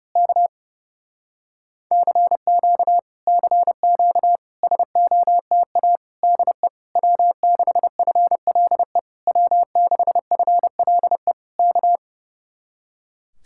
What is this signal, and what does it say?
Morse code, heard as a keyed tone, about 27 wpm, 700 Hz. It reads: K CQ CQ SOTA DE W6FLE W6FLE K
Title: Speed-Racing Common QSO Phrases - 30wpm